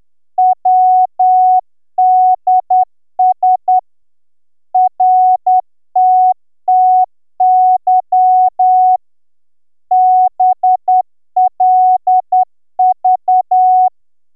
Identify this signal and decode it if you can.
Morse code, heard as a keyed tone, about 10 wpm, 740 Hz. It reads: WDS RTTY BLV